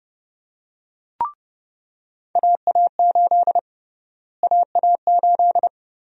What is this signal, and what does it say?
Morse code, heard as a keyed tone, about 30 wpm, 700 Hz. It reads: AA8 AA8
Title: USA Call Sign Prefixes - Repeated 2x - 30wpm